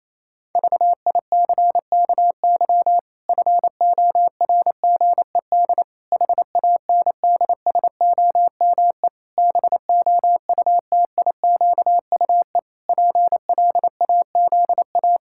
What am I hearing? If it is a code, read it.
Morse code, heard as a keyed tone, about 27 wpm, 700 Hz. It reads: VICKY FORGED HANDSOME BOUTIQUE PLAZA
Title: Pangram Challenge v2 - 28wpm